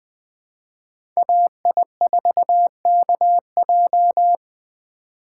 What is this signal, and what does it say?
Morse code, heard as a keyed tone, about 20 wpm, 700 Hz. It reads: AI4KJ
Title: USA Call Signs - 5 Characters - 4x Character Spacing - 20wpm